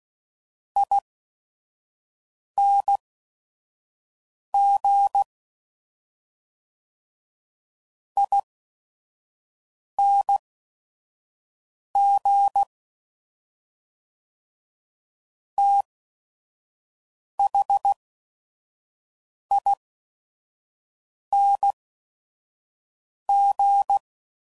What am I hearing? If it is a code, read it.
Morse code, heard as a keyed tone, about 16 wpm, spaced as for 5 wpm, 800 Hz. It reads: ING ING THING
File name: Suffixes / Special ING's / SlowToFaster ING's